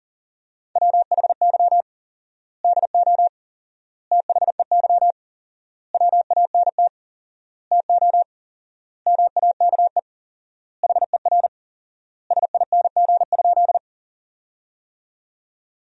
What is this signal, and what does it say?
Morse code, heard as a keyed tone, about 37 wpm, 700 Hz. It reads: WHY DO THEY WANT TO MAKE HER SING?
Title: 4x Word-Spacing - Sentences from Top 400 Words - 40wpm